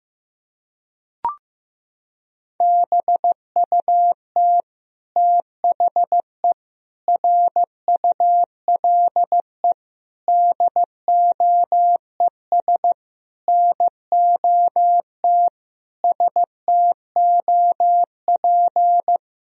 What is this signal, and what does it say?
Morse code, heard as a keyed tone, about 16 wpm, 700 Hz. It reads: BUT THE RULE DOES NOT STOP
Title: Sentences from Top 1000 Words - Review Part 2 - 15wpm